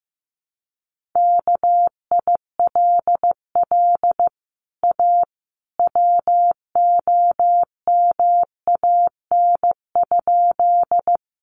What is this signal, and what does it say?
Morse code, heard as a keyed tone, about 14 wpm, 700 Hz. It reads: KILL A WOMAN?
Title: Sentences from Top 1400 Words 15wpm